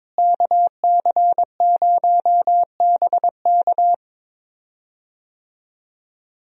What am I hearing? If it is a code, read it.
Morse code, heard as a keyed tone, about 22 wpm, 700 Hz. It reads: KC0BK
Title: USA Call Signs - 5 Characters - Repeated 3x - 22wpm